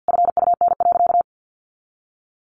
Morse code, heard as a keyed tone, about 50 wpm, 700 Hz. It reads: FUN.
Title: Additional Easy Phrases 50wpm